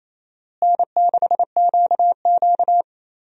Morse code, heard as a keyed tone, about 28 wpm, 700 Hz. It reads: N6QQ
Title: USA Call Signs - 4 Characters - Repeated 2x - 28wpm